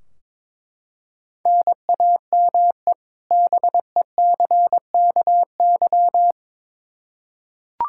Morse code, heard as a keyed tone, about 22 wpm, 700 Hz. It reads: NAME BECKY
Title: K1USN SST QSO Practice 22wpm